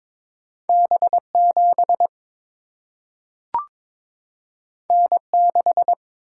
Morse code, heard as a keyed tone, about 22 wpm, 700 Hz. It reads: B7 N6
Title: Single-Letter Single-Number 22wpm